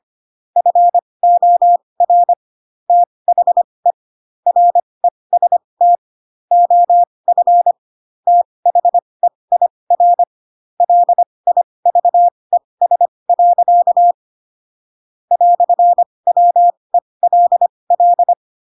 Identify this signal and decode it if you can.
Morse code, heard as a keyed tone, about 25 wpm, spaced as for 20 wpm, 700 Hz. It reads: FOR THE REST OF THEIR LIVES. "WELL
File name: Pinocchio / Pinocchio ch2 15 25 ik1hge